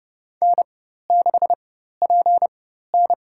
Morse code, heard as a keyed tone, about 30 wpm, spaced as for 14 wpm, 700 Hz. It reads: N6PN
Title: USA Call Signs - 4 Characters - 4x Character Spacing - 30wpm